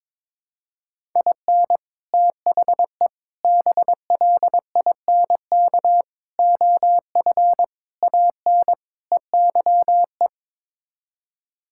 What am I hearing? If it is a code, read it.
Morse code, heard as a keyed tone, about 22 wpm, 700 Hz. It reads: IN THE BLINK OF AN EYE
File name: practice / 22wpm / Sets of 6 Words - 22wpm